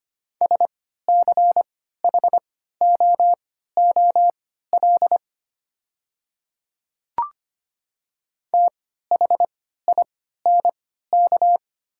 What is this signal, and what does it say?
Morse code, heard as a keyed tone, about 25 wpm, spaced as for 14 wpm, 690 Hz. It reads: SCHOOL THINK
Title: 3x Character Spacing - Top 500 Words - 25wpm